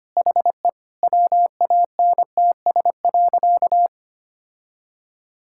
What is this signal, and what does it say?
Morse code, heard as a keyed tone, about 25 wpm, 700 Hz. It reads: HE WANTS.
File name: practice / 25wpm / Sentences with 2-Word Phrases - 25wpm